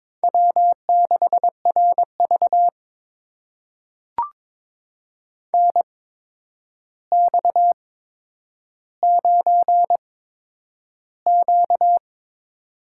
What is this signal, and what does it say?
Morse code, heard as a keyed tone, about 21 wpm, spaced as for 9 wpm, 700 Hz. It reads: W6RV NX9Q
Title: USA Call Signs - 4 Characters - 8x Character Spacing - 22wpm